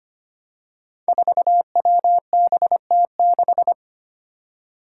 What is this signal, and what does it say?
Morse code, heard as a keyed tone, about 25 wpm, 700 Hz. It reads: 4WBT6
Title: Five Challenging Letter-Number Combinations 25wpm